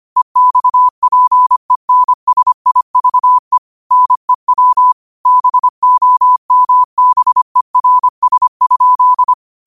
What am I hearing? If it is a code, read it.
Morse code, heard as a keyed tone, about 25 wpm, 1000 Hz. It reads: EXPENSIVE NEW BOMBERS?